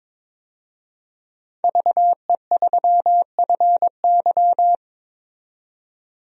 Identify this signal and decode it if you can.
Morse code, heard as a keyed tone, about 22 wpm, 700 Hz. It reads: VE3FY
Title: DXCC-100 Call Signs - 5 Characters - 22wpm